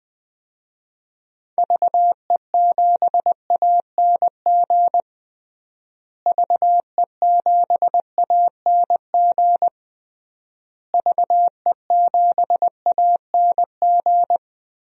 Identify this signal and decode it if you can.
Morse code, heard as a keyed tone, about 20 wpm, 700 Hz. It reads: VE7ANG VE7ANG VE7ANG
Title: DXCC-100 Call Signs - 6 Characters - Repeated 3x - 20wpm